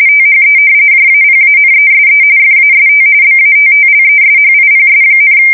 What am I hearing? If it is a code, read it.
RTTY, about 45.45 baud. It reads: THE JAPANESE FAST BATTLESHIPS,
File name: fsk170 / battleshipnj